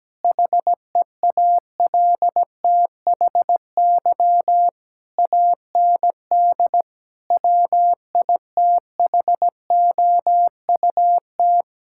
Morse code, heard as a keyed tone, about 17 wpm, 700 Hz. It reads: HEALTHY AND WITHOUT
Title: Sentences from Top 1000 Words - Review Part 6 - 17wpm